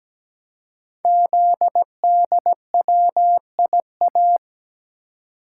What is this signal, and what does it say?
Morse code, heard as a keyed tone, about 17 wpm, 700 Hz. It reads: ZDWIA
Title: Code Groups - 5 Letters - 17wpm